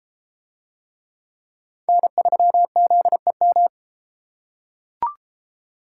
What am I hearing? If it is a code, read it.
Morse code, heard as a keyed tone, about 33 wpm, 700 Hz. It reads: N3ZEM